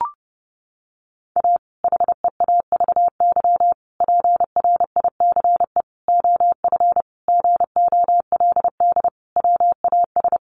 Morse code, heard as a keyed tone, about 30 wpm, 700 Hz. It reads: A HEAVY PRICE OF GOLD WAS
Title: Sentences from Top 1100 Words 30wpm